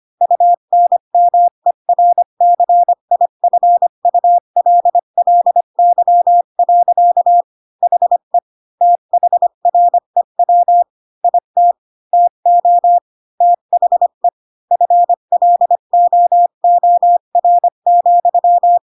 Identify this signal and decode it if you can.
Morse code, heard as a keyed tone, about 25 wpm, 700 Hz. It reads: UNMERCIFULLY. HE THREW IT TO THE FLOOR,
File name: Pinocchio / Pinocchio ch1 20 25 ik1hge